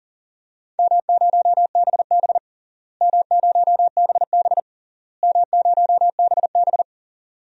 Morse code, heard as a keyed tone, about 40 wpm, 700 Hz. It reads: M0BB M0BB M0BB
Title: DXCC-100 Call Signs - 4 Characters - Repeated 3x - 40wpm